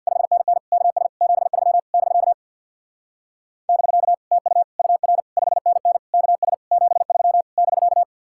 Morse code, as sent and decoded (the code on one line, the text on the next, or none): ..... -. -. / -.- ... / --... ...-- / -...-.- / -...-.- / - ..- / ..- .-. / ..... -. -. / -.- ... / --... ...-- / -...-.-
5NN KS 73 <BK> <BK> TU UR 5NN KS 73 <BK>